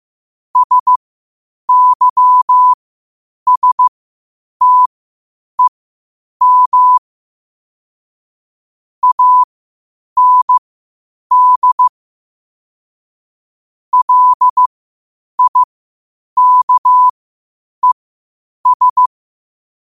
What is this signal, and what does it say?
Morse code, heard as a keyed tone, about 15 wpm, spaced as for 8 wpm, 1000 Hz. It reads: SYSTEM AND LIKES
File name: qotdfiles / 8wpm